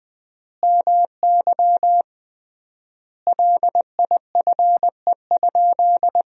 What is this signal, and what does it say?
Morse code, heard as a keyed tone, about 20 wpm, 700 Hz. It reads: MY LIFE?
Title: 3x Word-Spacing - Sentences from Top 300 Words - 20wpm